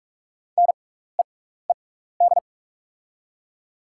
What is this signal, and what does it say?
Morse code, heard as a keyed tone, about 43 wpm, spaced as for 15 wpm, 700 Hz. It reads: NEED
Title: 6x Character Spacing - Top 500 Words - 45wpm